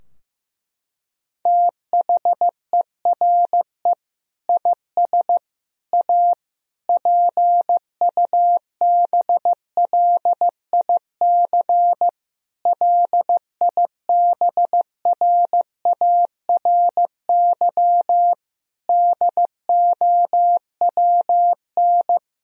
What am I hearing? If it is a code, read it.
Morse code, heard as a keyed tone, about 15 wpm, 700 Hz. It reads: THERE IS A PUBLIC LIBRARY DOWN
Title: Sentences from Top 1000 Words - Review Part 4 - 15wpm